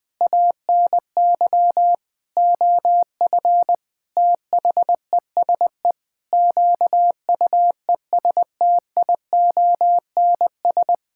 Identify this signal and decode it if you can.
Morse code, heard as a keyed tone, about 20 wpm, 700 Hz. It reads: ANY OF THESE QUESTIONS
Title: Sentences with 9-Word Phrases - 20wpm